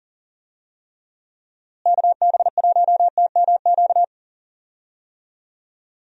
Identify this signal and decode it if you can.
Morse code, heard as a keyed tone, about 40 wpm, 700 Hz. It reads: KB1TMQ
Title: USA Call Signs - 6 Characters - Repeated 2x - 40wpm